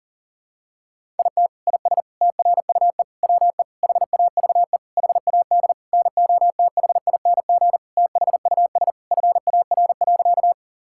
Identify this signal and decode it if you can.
Morse code, heard as a keyed tone, about 40 wpm, 700 Hz. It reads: IT IS TRUE WE HAVE HAD NOTHING THUS FAR.
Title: Sentences from Top 1000 Words - Review Part 5 - 40wpm